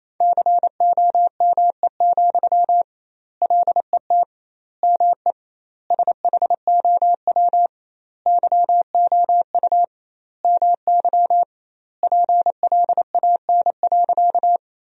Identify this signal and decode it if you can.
Morse code, heard as a keyed tone, about 28 wpm, 700 Hz. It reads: COME, LET ME SHOW YOU MY PLAN.
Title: 2x Word-Spacing - Sentences from Top 500 Words - 28wpm